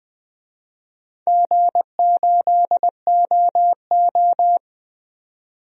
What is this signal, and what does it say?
Morse code, heard as a keyed tone, about 20 wpm, 700 Hz. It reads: G8OO
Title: DXCC-100 Call Signs - 4 Characters - 20wpm